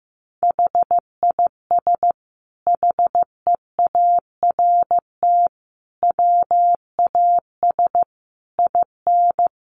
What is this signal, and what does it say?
Morse code, heard as a keyed tone, about 15 wpm, 700 Hz. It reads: HIS HEART WAS IN
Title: Sentences from Top 1400 Words 15wpm